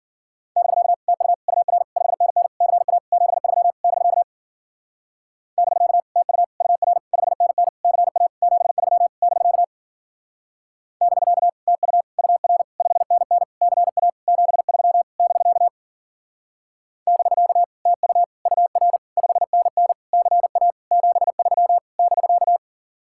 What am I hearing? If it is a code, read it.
Morse code, heard as a keyed tone, about 50 wpm, 700 Hz. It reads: <BK> TU UR 5NN CA 73 <BK> <BK> TU UR 5NN CA 73 <BK> <BK> TU UR 5NN CA 73 <BK> <BK> TU UR 5NN CA 73 <BK>
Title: Speed-Racing Common QSO Phrases Encore - 40wpm